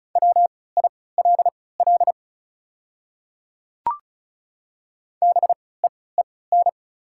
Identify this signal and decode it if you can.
Morse code, heard as a keyed tone, about 36 wpm, spaced as for 20 wpm, 700 Hz. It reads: WILL BEEN
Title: 3x Character Spacing - Top 100 Words - 35wpm